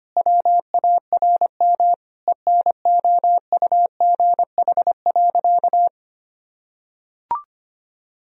Morse code, heard as a keyed tone, about 25 wpm, 700 Hz. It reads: WARM ENOUGH.
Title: Sentences from Top 1000 Words - Review Part 2 - 25wpm